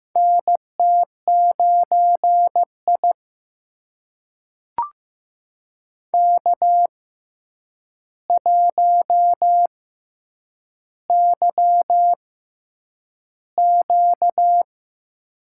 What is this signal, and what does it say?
Morse code, heard as a keyed tone, about 15 wpm, spaced as for 8 wpm, 700 Hz. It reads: NT9I K1YQ
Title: USA Call Signs - 4 Characters - 6x Character Spacing - 15wpm